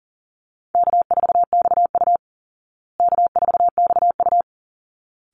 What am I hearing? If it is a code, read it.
Morse code, heard as a keyed tone, about 40 wpm, 700 Hz. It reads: K4XU K4XU
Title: Easy Call Signs Repeated 2x - 40wpm